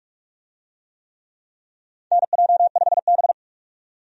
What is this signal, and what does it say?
Morse code, heard as a keyed tone, about 46 wpm, 700 Hz. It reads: NJ5B
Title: USA Call Signs - 4 Characters - 45wpm